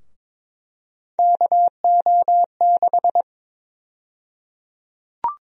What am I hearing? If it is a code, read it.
Morse code, heard as a keyed tone, about 22 wpm, 700 Hz. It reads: KO6
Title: USA Call Sign Prefixes - Repeated 2x - 22wpm